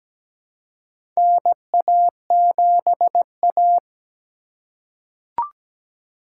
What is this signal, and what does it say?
Morse code, heard as a keyed tone, about 17 wpm, 700 Hz. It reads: NA7A
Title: USA Call Signs - 4 Characters - 3x Character Spacing - 17wpm